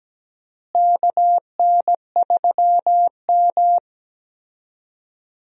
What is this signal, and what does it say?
Morse code, heard as a keyed tone, about 17 wpm, 700 Hz. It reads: KN3M
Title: USA Call Signs - 4 Characters - Repeated 2x - 17wpm